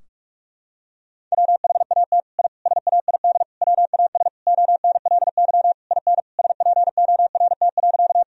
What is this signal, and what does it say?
Morse code, heard as a keyed tone, about 46 wpm, 700 Hz. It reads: WHAT I SAID WAS ONLY IN SPORT.
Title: Sentences from Top 1000 Words - Review Part 4 - 45wpm